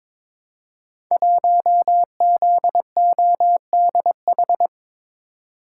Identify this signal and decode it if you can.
Morse code, heard as a keyed tone, about 22 wpm, 700 Hz. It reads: 1ZODH
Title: Code Groups - 5 Letter:Numbers - 22wpm